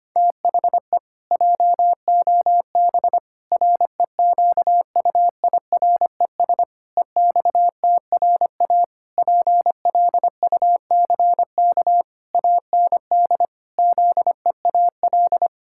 Morse code, heard as a keyed tone, about 25 wpm, 700 Hz. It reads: THE JOB REQUIRES EXTRA PLUCK AND ZEAL
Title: Pangram Challenge v2 - 25wpm